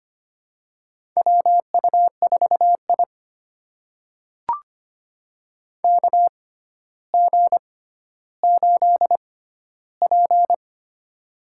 Morse code, heard as a keyed tone, about 25 wpm, spaced as for 13 wpm, 700 Hz. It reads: WU4I KG8P
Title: USA Call Signs - 4 Characters - 6x Character Spacing - 25wpm